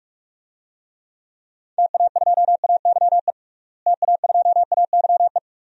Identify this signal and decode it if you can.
Morse code, heard as a keyed tone, about 41 wpm, 700 Hz. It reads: TA2AYE TA2AYE
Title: DXCC-100 Call Signs - 6 Characters - Repeated 2x - 45wpm